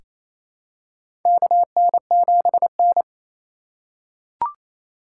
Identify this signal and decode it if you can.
Morse code, heard as a keyed tone, about 27 wpm, 700 Hz. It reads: KN7N
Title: USA Call Signs - 4 Characters - 2x Character Spacing - 28wpm